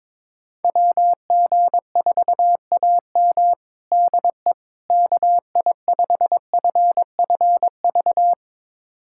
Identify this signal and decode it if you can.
Morse code, heard as a keyed tone, about 21 wpm, 700 Hz. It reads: WG4AM DE KI5FFV
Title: USA Callsign DE Callsign 22wpm